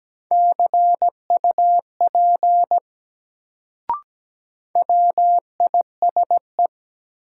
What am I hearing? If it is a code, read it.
Morse code, heard as a keyed tone, about 17 wpm, 700 Hz. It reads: CUP WISE
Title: Sentences from Top 1600 Words - New Word Drill Alt - 17wpm